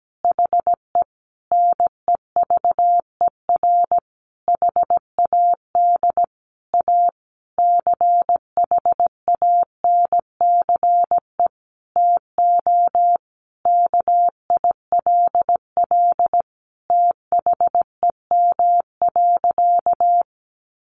Morse code, heard as a keyed tone, about 17 wpm, 700 Hz. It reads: HE NEVER HAD A CHANCE TO KILL THEM.
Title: Sentences from Top 1400 Words 17wpm